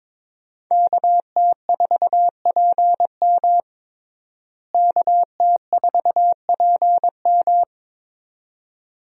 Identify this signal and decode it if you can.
Morse code, heard as a keyed tone, about 22 wpm, 700 Hz. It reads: KT4PM KT4PM
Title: USA Call Signs - 5 Characters - Repeated 2x - 22wpm